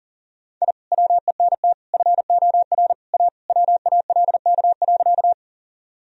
Morse code, heard as a keyed tone, about 40 wpm, 700 Hz. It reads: I WENT FOR A WALK.
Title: Sentences from Top 1000 Words - Review Part 1 - 40wpm